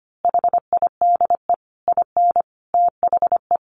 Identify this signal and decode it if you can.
Morse code, heard as a keyed tone, about 25 wpm, 700 Hz. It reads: HIDE IN THE
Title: A Sound Of Thunder 25wpm